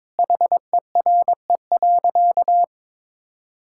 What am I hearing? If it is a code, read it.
Morse code, heard as a keyed tone, about 22 wpm, 700 Hz. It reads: HERE.
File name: practice / 22wpm / Sentences with 8-Word Phrases - 22wpm